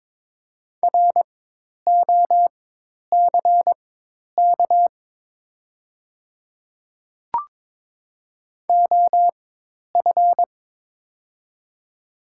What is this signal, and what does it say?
Morse code, heard as a keyed tone, about 22 wpm, spaced as for 10 wpm, 700 Hz. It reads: ROCK OF
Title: 4x Character Spacing - Top 500 Words - 22wpm